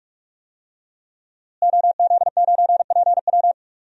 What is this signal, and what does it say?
Morse code, heard as a keyed tone, about 46 wpm, 700 Hz. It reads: OZ9PW
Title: DXCC-100 Call Signs - 5 Characters - Repeated 2x - 45wpm